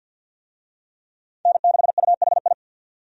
Morse code, heard as a keyed tone, about 50 wpm, 700 Hz. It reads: N6UHI
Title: USA Call Signs - 5 Characters - 50wpm